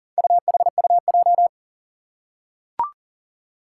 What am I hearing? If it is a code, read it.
Morse code, heard as a keyed tone, about 40 wpm, 700 Hz. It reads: UHUJ
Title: Four Challenging Letter-Number Combinations 40wpm